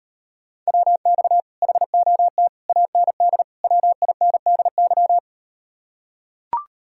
Morse code, heard as a keyed tone, about 38 wpm, 700 Hz. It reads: WX HOT AND WINDY